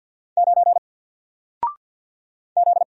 Morse code, heard as a keyed tone, about 48 wpm, 700 Hz. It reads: 9 Z
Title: Single Character (Letter:Number:Punctuation:Prosign) Alt 50wpm